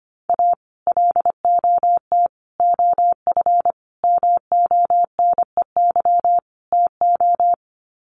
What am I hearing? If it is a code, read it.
Morse code, heard as a keyed tone, about 25 wpm, 700 Hz. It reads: A LOT OF MONEY TO
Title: Sets of 5 Words Encore2 25wpm